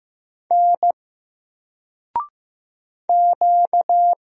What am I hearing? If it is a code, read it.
Morse code, heard as a keyed tone, about 15 wpm, 700 Hz. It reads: N Q
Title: Single Letter-Number 15wpm